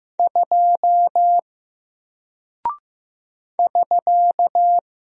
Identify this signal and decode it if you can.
Morse code, heard as a keyed tone, about 15 wpm, 700 Hz. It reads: 2 <SK>
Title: Single Character (Letter:Number:Punctuation:Prosign) Alt 15wpm